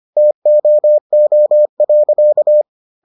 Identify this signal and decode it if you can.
Morse code, heard as a keyed tone, about 25 wpm, 600 Hz. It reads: TOO.